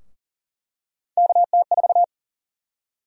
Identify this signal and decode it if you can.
Morse code, heard as a keyed tone, about 39 wpm, 700 Hz. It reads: KT4
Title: USA Call Sign Prefixes - 40wpm